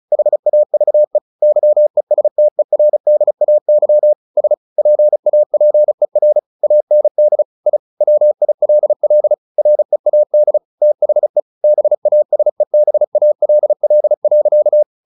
Morse code, heard as a keyed tone, about 35 wpm, 600 Hz. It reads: HAVE YESTERDAY S PAPER AND I WILL READ THE BASEBALL.
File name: the old man and the sea / the old man and the sea 1 35wpm